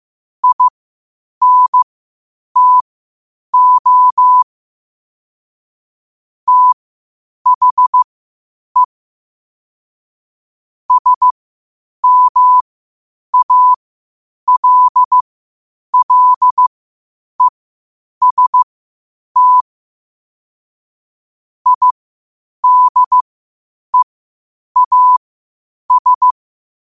Morse code, heard as a keyed tone, about 15 wpm, spaced as for 8 wpm, 1000 Hz. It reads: INTO THE SMALLEST IDEAS
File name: qotdfiles / 8wpm